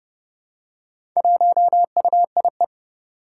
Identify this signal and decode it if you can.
Morse code, heard as a keyed tone, about 30 wpm, 700 Hz. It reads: 1UIE
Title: Four Challenging Letter-Number Combinations 30wpm